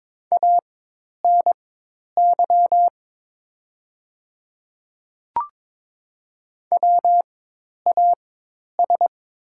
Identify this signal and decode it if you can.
Morse code, heard as a keyed tone, about 22 wpm, spaced as for 10 wpm, 700 Hz. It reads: ANY WAS
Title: 4x Character Spacing - Top 200 Words - 22wpm